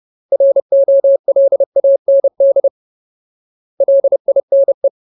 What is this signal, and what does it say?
Morse code, heard as a keyed tone, about 30 wpm, 550 Hz. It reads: ROLAND LINE
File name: Ultimate Word List / 30WPM Ultimate